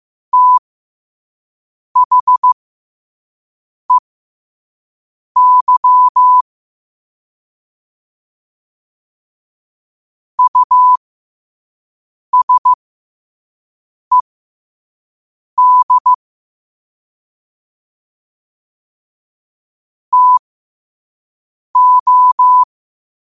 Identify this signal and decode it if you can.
Morse code, heard as a keyed tone, about 15 wpm, spaced as for 5 wpm, 1000 Hz. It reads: THEY USED TO
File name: 5wpm